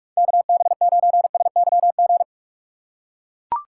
Morse code, heard as a keyed tone, about 41 wpm, 690 Hz. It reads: KB9SYG
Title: USA Call Signs - 6 Characters - 3x Character Spacing - 45wpm